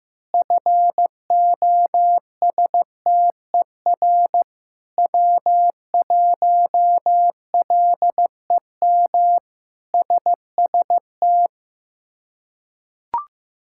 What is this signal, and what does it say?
Morse code, heard as a keyed tone, about 15 wpm, 700 Hz. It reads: FOSTER W1LEM SST
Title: K1USN SST QSO Practice - 15wpm